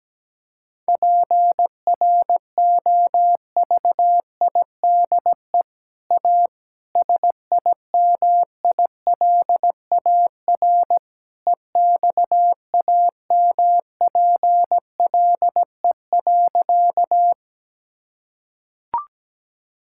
Morse code, heard as a keyed tone, about 17 wpm, 700 Hz. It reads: PROVIDE A SIMILAR EXAMPLE.
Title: Sentences from Top 1000 Words - Review Part 5 - 17wpm